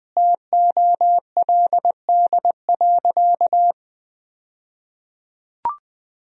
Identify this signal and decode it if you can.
Morse code, heard as a keyed tone, about 20 wpm, 700 Hz. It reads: TOLD.
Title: 2x Word-Spacing - Sentences from Top 400 Words - 20wpm